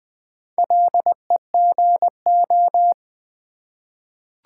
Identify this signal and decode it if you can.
Morse code, heard as a keyed tone, about 20 wpm, 690 Hz. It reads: LEGO
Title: Difficult Words with Hints - 20wpm